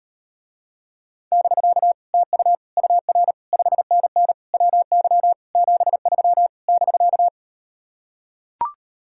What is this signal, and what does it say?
Morse code, heard as a keyed tone, about 38 wpm, 700 Hz. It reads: <BK> TU UR 5NN WY 73 <BK>